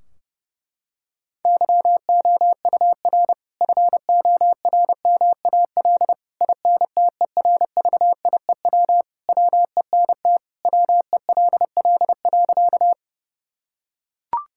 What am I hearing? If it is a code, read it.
Morse code, heard as a keyed tone, about 30 wpm, 700 Hz. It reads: YOUR FORMAL INTERVIEW WENT WELL.
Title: Sentences from Top 1600 Words 30wpm